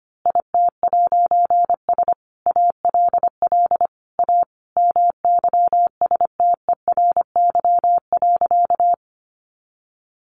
Morse code, heard as a keyed tone, about 25 wpm, 700 Hz. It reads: IT'S ALL A MYSTERY.
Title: A Sound Of Thunder 25wpm